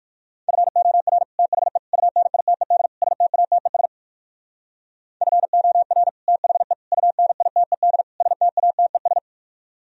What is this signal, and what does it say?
Morse code, heard as a keyed tone, about 58 wpm, 700 Hz. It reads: FOR THE UNITED STATES FOR THE UNITED STATES